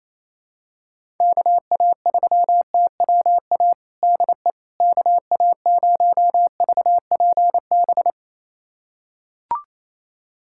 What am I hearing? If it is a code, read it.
Morse code, heard as a keyed tone, about 28 wpm, 700 Hz. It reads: KA3TWA DE KA0VPB
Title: USA Callsign DE Callsign - 6 Characters - 28wpm